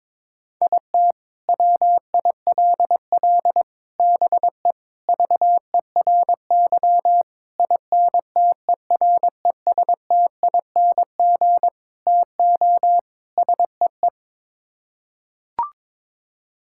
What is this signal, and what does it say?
Morse code, heard as a keyed tone, about 22 wpm, 700 Hz. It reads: IT WILL BE VERY INTERESTING TO SEE
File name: practice / 22wpm / Sets of 7 Words - 22wpm